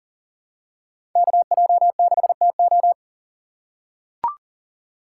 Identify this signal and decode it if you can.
Morse code, heard as a keyed tone, about 40 wpm, 700 Hz. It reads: KJ6TO
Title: USA Call Signs - 2x Character Spacing - 40wpm